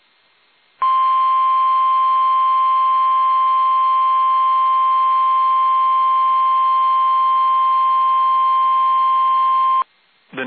Emergency Alert System (SAME) tones (the 1050 Hz attention tone). It is heard from 0.8 to 9.9 s.
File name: eas / KUSO / KUSO